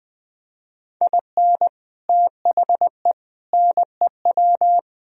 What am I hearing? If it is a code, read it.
Morse code, heard as a keyed tone, about 20 wpm, 700 Hz. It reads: IN THE NEW
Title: Speed-Racing - Sets of 5 Words 17wpm